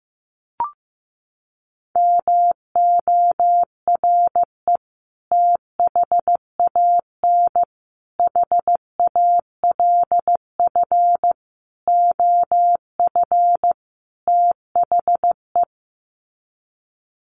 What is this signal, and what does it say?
Morse code, heard as a keyed tone, about 15 wpm, 700 Hz. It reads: MORE THAN HALF OF THE
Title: Sets of 5 Words 15wpm